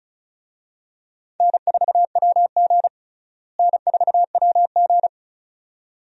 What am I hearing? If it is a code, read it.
Morse code, heard as a keyed tone, about 36 wpm, 700 Hz. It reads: N4WG N4WG
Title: USA Call Signs - 4 Characters - Repeated 2x - 35wpm